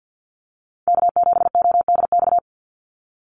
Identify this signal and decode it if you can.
Morse code, heard as a keyed tone, about 49 wpm, 700 Hz. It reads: K7ODX
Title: NA SOTA Chaser Callsigns 50wpm